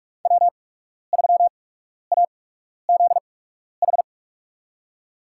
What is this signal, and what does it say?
Morse code, heard as a keyed tone, about 46 wpm, spaced as for 12 wpm, 700 Hz. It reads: W3AZH
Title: USA Call Signs - 5 Characters - 8x Character Spacing - 45wpm